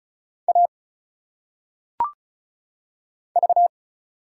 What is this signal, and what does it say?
Morse code, heard as a keyed tone, about 34 wpm, 700 Hz. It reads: A V
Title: Single Letters - Phonetic Pronunciation 35wpm